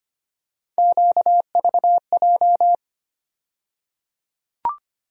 Morse code, heard as a keyed tone, about 25 wpm, 700 Hz. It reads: QVJ
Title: USA Call Sign Postfixes - 3 Letters 25wpm